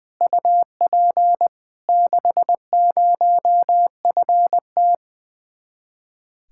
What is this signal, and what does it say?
Morse code, heard as a keyed tone, about 20 wpm, 700 Hz. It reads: UP 60FT
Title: Speed-Racing Common QSO Phrases - 20wpm